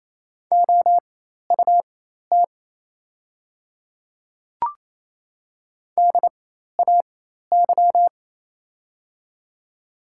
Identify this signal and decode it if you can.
Morse code, heard as a keyed tone, about 28 wpm, spaced as for 13 wpm, 700 Hz. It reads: OUT DAY
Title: 4x Character Spacing - Top 100 Words - 28wpm